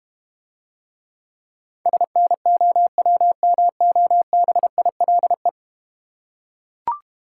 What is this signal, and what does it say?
Morse code, heard as a keyed tone, about 31 wpm, 700 Hz. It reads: SNOWMOBILE
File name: practice / 25wpm / Difficult Words - Speed-Racing with Hints - 25wpm